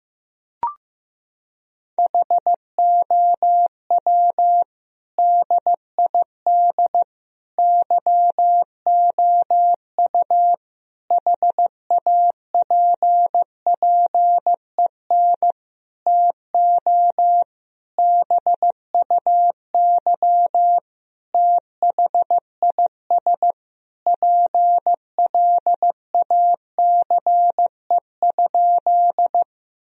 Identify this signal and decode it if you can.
Morse code, heard as a keyed tone, about 15 wpm, 700 Hz. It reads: HOW DID YOU HAPPEN TO BUY THIS PLACE?
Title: Sentences from Top 1000 Words - Review Part 4 - 15wpm